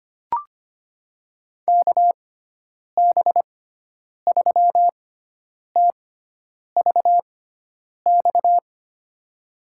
Morse code, heard as a keyed tone, about 25 wpm, spaced as for 9 wpm, 700 Hz. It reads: KB3TVX
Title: USA Call Signs - 6 Characters - 6x Character Spacing - 25wpm